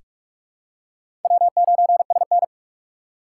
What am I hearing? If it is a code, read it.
Morse code, heard as a keyed tone, about 46 wpm, 700 Hz. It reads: W9SN
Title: USA Call Signs - 4 Characters - 3x Character Spacing - 45wpm